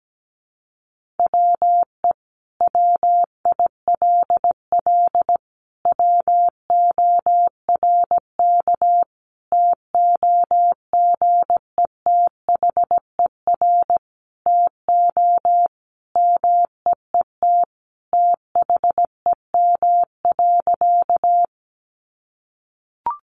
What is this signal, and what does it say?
Morse code, heard as a keyed tone, about 17 wpm, 700 Hz. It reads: WE WILL WORK TOGETHER TO MEET THEM.
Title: Sentences from Top 1100 Words 17wpm